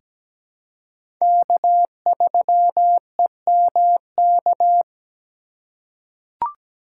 Morse code, heard as a keyed tone, about 17 wpm, 700 Hz. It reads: K3EMK
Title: USA Call Signs - 5 Characters - 4x Character Spacing - 17wpm